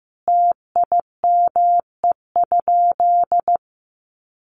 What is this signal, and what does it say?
Morse code, heard as a keyed tone, about 15 wpm, 700 Hz. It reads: TIME?
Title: Sentences from Top 100 Words 15wpm